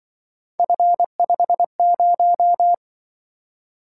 Morse code, heard as a keyed tone, about 23 wpm, 700 Hz. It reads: F50
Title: Speed-Racing - Three Letter-Number Combinations - 20wpm